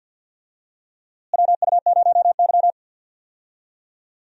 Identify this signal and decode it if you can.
Morse code, heard as a keyed tone, about 52 wpm, 700 Hz. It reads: WU0Y
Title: USA Call Signs - 4 Characters - Repeated 2x - 50wpm